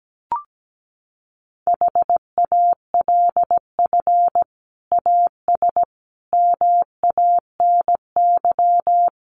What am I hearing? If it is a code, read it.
Morse code, heard as a keyed tone, about 17 wpm, 700 Hz. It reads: HALF AS MANY
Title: Sentences from Top 400 Words 17wpm